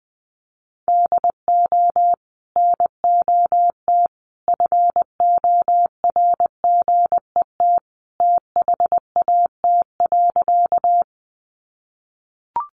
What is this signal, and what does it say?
Morse code, heard as a keyed tone, about 20 wpm, 700 Hz. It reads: DO NOT FORGET THAT.
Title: Sentences from Top 1200 Words 20wpm